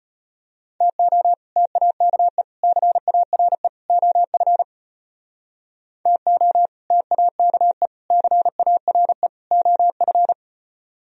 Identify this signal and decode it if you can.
Morse code, heard as a keyed tone, about 32 wpm, 700 Hz. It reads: TO TAKE CARE OF TO TAKE CARE OF